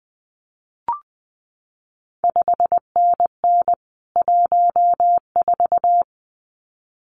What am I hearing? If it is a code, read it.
Morse code, heard as a keyed tone, about 19 wpm, 700 Hz. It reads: 5NN 14
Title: CQ WW DX Contest Exchange - 20wpm